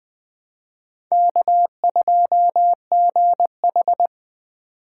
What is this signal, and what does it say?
Morse code, heard as a keyed tone, about 20 wpm, 700 Hz. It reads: K2GH
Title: Speed Racing - USA 1x2 Call Signs 17wpm